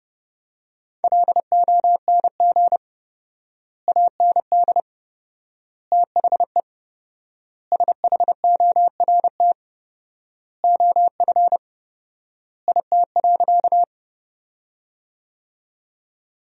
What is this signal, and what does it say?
Morse code, heard as a keyed tone, about 32 wpm, 700 Hz. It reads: LONG AND THE SHORT OF IT.
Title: 4x Word-Spacing - Sentences from Top 400 Words - 30wpm